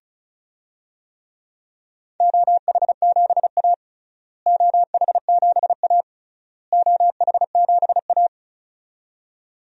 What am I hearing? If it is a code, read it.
Morse code, heard as a keyed tone, about 34 wpm, 700 Hz. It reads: OH7A OH7A OH7A
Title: DXCC-100 Call Signs - 4 Characters - Repeated 3x - 35wpm